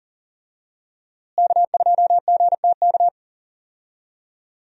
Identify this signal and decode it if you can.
Morse code, heard as a keyed tone, about 40 wpm, 700 Hz. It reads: K2GTK
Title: USA Call Signs - 5 Characters - Repeated 3x - 40wpm